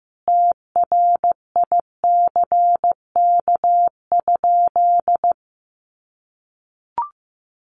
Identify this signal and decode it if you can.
Morse code, heard as a keyed tone, about 15 wpm, 700 Hz. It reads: TRICK?
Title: Sentences from Top 1500 Words 15wpm